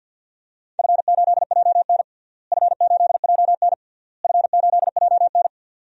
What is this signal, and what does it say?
Morse code, heard as a keyed tone, about 47 wpm, 700 Hz. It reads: F8JN F8JN F8JN
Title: DXCC-100 Call Signs - 4 Characters - Repeated 3x - 50wpm